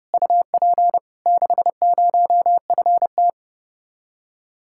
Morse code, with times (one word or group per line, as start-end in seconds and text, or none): 0.1-1.0 UP
1.3-3.3 60FT